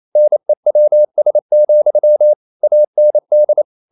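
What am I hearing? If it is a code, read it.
Morse code, heard as a keyed tone, about 28 wpm, 600 Hz. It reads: NEWS, AND
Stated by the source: Morse code practice at 28 wpm, text of Ken Follett's novel Never in English, from CW med Gnister.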